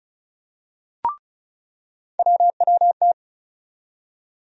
Morse code, heard as a keyed tone, about 36 wpm, 700 Hz. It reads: WWT
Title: USA Call Sign Postfixes - 3 Letters 35wpm